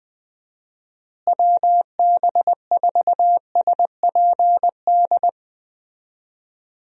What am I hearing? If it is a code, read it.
Morse code, heard as a keyed tone, about 20 wpm, 700 Hz. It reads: WB4SPD
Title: USA Call Signs - 6 Characters - Repeated 2x - 20wpm